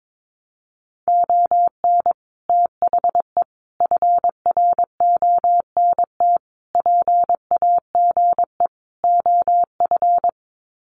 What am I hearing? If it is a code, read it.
Morse code, heard as a keyed tone, about 22 wpm, 700 Hz. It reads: ON THE FRONT PAGE OF
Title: Sets of 5 Words Encore 22wpm